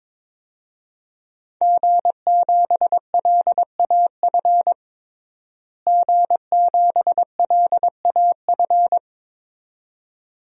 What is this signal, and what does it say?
Morse code, heard as a keyed tone, about 22 wpm, 700 Hz. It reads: G7LAF G7LAF
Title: DXCC-100 Call Signs - 5 Characters - Repeated 2x - 22wpm